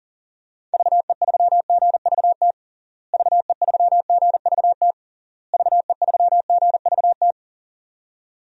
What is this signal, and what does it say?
Morse code, heard as a keyed tone, about 37 wpm, 700 Hz. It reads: VE3GVT VE3GVT VE3GVT
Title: DXCC-100 Call Signs - 6 Characters - Repeated 3x - 40wpm